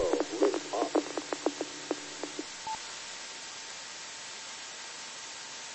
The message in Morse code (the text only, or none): none